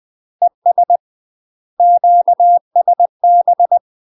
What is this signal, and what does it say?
Morse code, heard as a keyed tone, about 20 wpm, 700 Hz. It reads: ES QSB
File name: UK MASTER QSOs TRAINING/20WPM